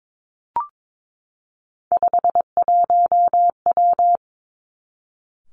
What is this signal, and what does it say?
Morse code, heard as a keyed tone, about 22 wpm, 700 Hz. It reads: 51W
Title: Three Challenging Letter-Number Combinations 22wpm